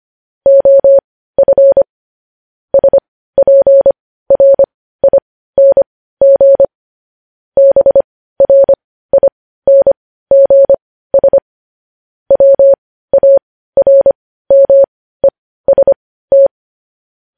Morse code, heard as a keyed tone, about 25 wpm, spaced as for 15 wpm, 550 Hz. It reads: OF SPRING BRINGS WARMEST
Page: news headlines in morse code at 15 wpm